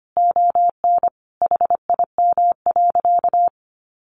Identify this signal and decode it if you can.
Morse code, heard as a keyed tone, about 25 wpm, 700 Hz. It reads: ON HIM.
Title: Sentences from Top 1500 Words 25wpm